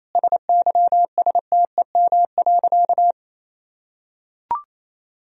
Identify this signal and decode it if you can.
Morse code, heard as a keyed tone, about 27 wpm, 700 Hz. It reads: SYSTEM.
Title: Sentences from Top 1000 Words - Review Part 3 - 28wpm